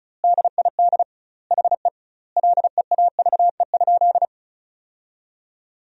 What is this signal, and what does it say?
Morse code, heard as a keyed tone, about 36 wpm, 700 Hz. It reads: DID HE LEAVE?
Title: 2x Word-Spacing - Sentences from Top 400 Words - 35wpm